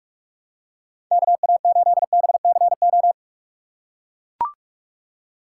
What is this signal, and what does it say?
Morse code, heard as a keyed tone, about 41 wpm, 700 Hz. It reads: KA8BCO
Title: USA Call Signs - 6 Characters - 3x Character Spacing - 45wpm